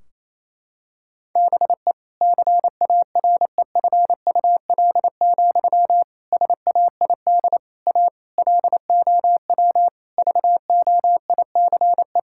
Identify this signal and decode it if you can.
Morse code, heard as a keyed tone, about 28 wpm, 700 Hz. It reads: BE CAREFUL, SAID A LOW VOICE
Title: Sentences from Top 1600 Words 28wpm